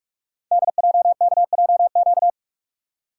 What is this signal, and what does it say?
Morse code, heard as a keyed tone, about 44 wpm, 700 Hz. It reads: DJKJQ
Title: Code Groups - 5 Letters - 45wpm